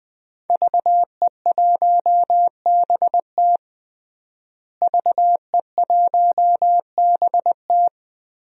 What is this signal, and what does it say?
Morse code, heard as a keyed tone, about 20 wpm, 700 Hz. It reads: VE1BT VE1BT
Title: DXCC-100 Call Signs - 5 Characters - Repeated 2x - 20wpm